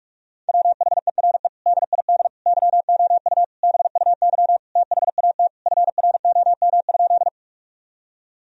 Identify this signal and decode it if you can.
Morse code, heard as a keyed tone, about 46 wpm, 700 Hz. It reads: WHERE DID YOU BUY THAT FROM?
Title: Sentences from Top 1000 Words - Review Part 4 - 45wpm